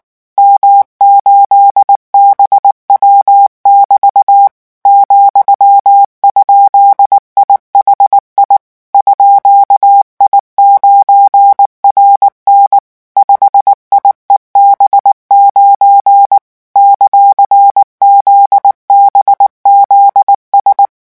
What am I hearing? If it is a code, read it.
Morse code, heard as a keyed tone, about 19 wpm, 800 Hz. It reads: M8BW<BT> ,?IHI _I9RN 5IEB9 ;ZBZS
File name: morse mp3 / revision17-vitesse 18 mots